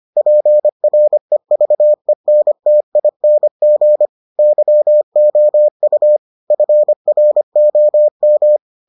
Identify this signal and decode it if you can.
Morse code, heard as a keyed tone, about 25 wpm, 600 Hz. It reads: PREVENTING YOU FROM